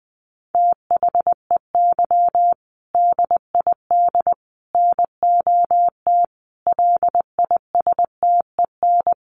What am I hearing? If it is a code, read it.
Morse code, heard as a keyed tone, about 20 wpm, 700 Hz. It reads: THEY DID NOT LISTEN
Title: Sentences from Top 600 Words 20wpm